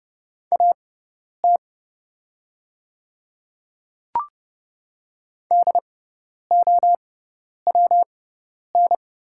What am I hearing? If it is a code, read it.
Morse code, heard as a keyed tone, about 30 wpm, spaced as for 10 wpm, 690 Hz. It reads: AT DOWN
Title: 6x Character Spacing - Top 200 Words - 30wpm